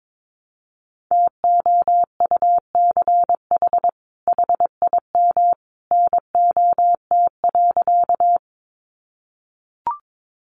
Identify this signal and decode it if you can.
Morse code, heard as a keyed tone, about 22 wpm, 700 Hz. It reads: TOUCH HIM NOT.
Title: Sentences from Top 1200 Words 22wpm